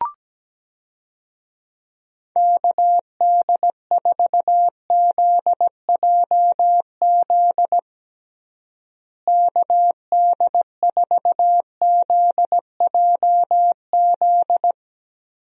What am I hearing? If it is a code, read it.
Morse code, heard as a keyed tone, about 17 wpm, 700 Hz. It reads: KD4ZJZ KD4ZJZ
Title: USA Call Signs - 6 Characters - Repeated 2x - 17wpm